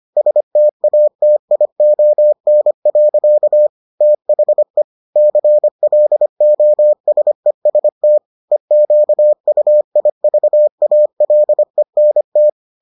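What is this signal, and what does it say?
Morse code, heard as a keyed tone, about 25 wpm, 600 Hz. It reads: STATION. THE CLOSEST EQUIVALENT
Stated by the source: Morse edition of the Foundations of Amateur Radio episodes.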